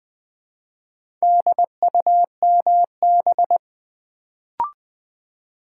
Morse code, heard as a keyed tone, about 20 wpm, 700 Hz. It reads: DUMB
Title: Build A Word - 2 Letters At A Time - 4-Letter Words - 20wpm